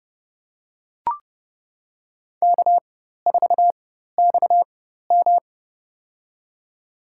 Morse code, heard as a keyed tone, about 29 wpm, spaced as for 14 wpm, 700 Hz. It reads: K4XM
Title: USA Call Signs - 4x Character Spacing - 30wpm